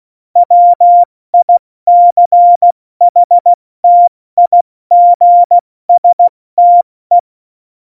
Morse code, heard as a keyed tone, about 16 wpm, 700 Hz. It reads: WICHTIGSTE
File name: Morsekurs / Witze / Witz 11. Juni 2019, 70 F80